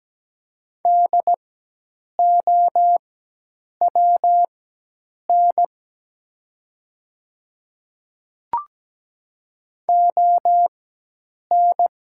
Morse code, heard as a keyed tone, about 17 wpm, spaced as for 8 wpm, 700 Hz. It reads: DOWN ON
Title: 4x Character Spacing - Top 100 Words - 17wpm